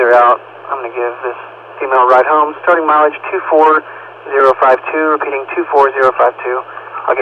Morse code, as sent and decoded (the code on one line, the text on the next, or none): none